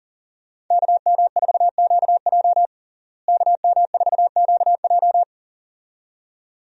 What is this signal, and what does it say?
Morse code, heard as a keyed tone, about 37 wpm, 700 Hz. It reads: KM4QJ KM4QJ
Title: USA Call Signs - 5 Characters - Repeated 2x - 40wpm